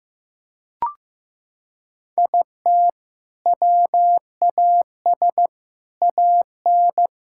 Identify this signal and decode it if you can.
Morse code, heard as a keyed tone, about 15 wpm, 700 Hz. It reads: IT WAS AN
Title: Sentences from Top 1000 Words - Review Part 5 - 15wpm